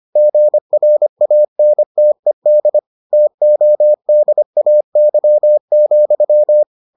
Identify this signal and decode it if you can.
Morse code, heard as a keyed tone, about 25 wpm, 600 Hz. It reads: GRANTED TODAY,